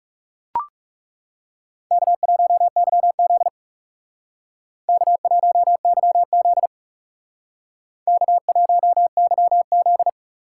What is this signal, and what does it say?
Morse code, heard as a keyed tone, about 38 wpm, 700 Hz. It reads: K1YZ K1YZ K1YZ